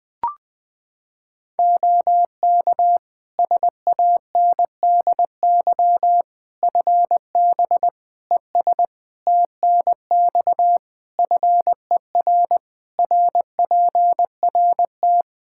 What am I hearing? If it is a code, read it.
Morse code, heard as a keyed tone, about 21 wpm, 700 Hz. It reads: OK SANDY FB ES TNX FER RPRT
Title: K1USN SST QSO Practice 20wpm